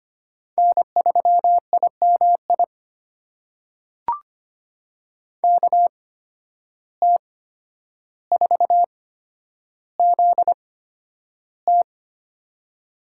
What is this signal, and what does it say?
Morse code, heard as a keyed tone, about 25 wpm, spaced as for 11 wpm, 700 Hz. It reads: N3IMI KT4ZT
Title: USA Call Signs - 5 Characters - 8x Character Spacing - 25wpm